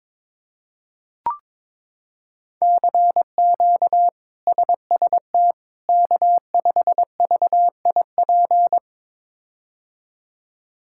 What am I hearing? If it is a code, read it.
Morse code, heard as a keyed tone, about 22 wpm, 700 Hz. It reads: CQ SST K5VIP
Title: K1USN SST QSO Practice - 22wpm